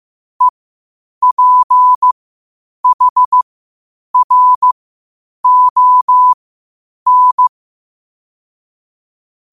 Morse code, heard as a keyed tone, about 15 wpm, spaced as for 8 wpm, 1000 Hz. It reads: EPHRON
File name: qotdfiles / 8wpm